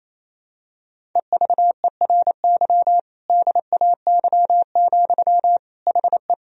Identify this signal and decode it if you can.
Morse code, heard as a keyed tone, about 28 wpm, 700 Hz. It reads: EVERY DAY, HE
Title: Sentences from Top 1000 Words - Review Part 1 - 28wpm